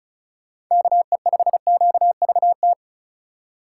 Morse code, heard as a keyed tone, about 36 wpm, 690 Hz. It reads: KE5QVT
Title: USA Call Signs - 6 Characters - 3x Character Spacing - 35wpm